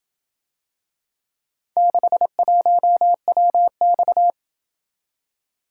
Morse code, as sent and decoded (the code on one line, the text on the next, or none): -.... .---- .-- -..-
61WX